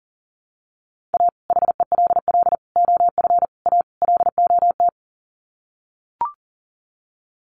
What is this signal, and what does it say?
Morse code, heard as a keyed tone, about 40 wpm, 700 Hz. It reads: A HELL OF A LOT
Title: Sets of 5 Words Encore 40wpm